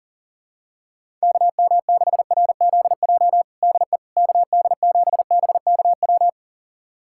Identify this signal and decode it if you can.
Morse code, heard as a keyed tone, about 37 wpm, 700 Hz. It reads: KM6RZJ DE KD7BKW
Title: USA Callsign DE Callsign - 6 Characters - 40wpm